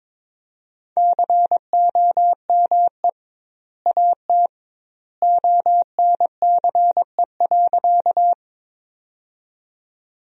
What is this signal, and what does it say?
Morse code, heard as a keyed tone, about 22 wpm, 700 Hz. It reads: COME AT ONCE.
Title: 2x Word-Spacing - Sentences from Top 300 Words - 22wpm